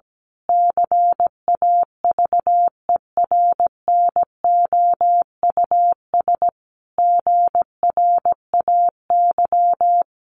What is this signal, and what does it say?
Morse code, heard as a keyed tone, about 17 wpm, 700 Hz. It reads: CAVERNOUS GRAY
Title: A Sound Of Thunder 17wpm